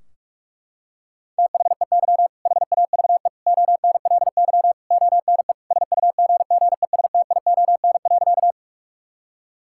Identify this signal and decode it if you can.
Morse code, heard as a keyed tone, about 41 wpm, 700 Hz. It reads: THEY HAVE ONLY ONE SUGGESTION.
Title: Sentences from Top 1600 Words 45wpm